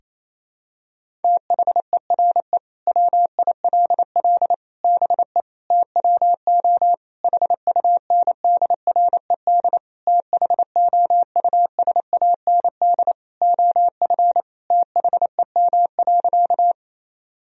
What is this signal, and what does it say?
Morse code, heard as a keyed tone, about 28 wpm, 700 Hz. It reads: THERE WILL BE TWO HUNDRED THOUSAND OF THEM.
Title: Sentences from Top 1000 Words - Review Part 3 - 28wpm